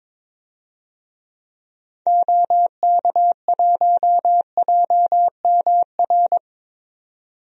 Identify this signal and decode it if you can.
Morse code, heard as a keyed tone, about 21 wpm, 700 Hz. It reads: OK1JMR
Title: DXCC-100 Call Signs - 6 Characters - 22wpm